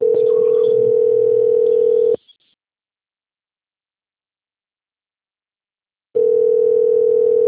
Ringback tone (North America), 440 + 480 Hz.